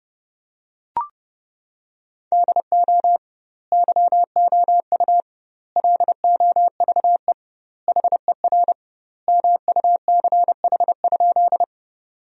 Morse code, heard as a keyed tone, about 30 wpm, 700 Hz. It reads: DO YOU LOVE HER MUCH?
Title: 2x Word-Spacing - Sentences from Top 400 Words - 30wpm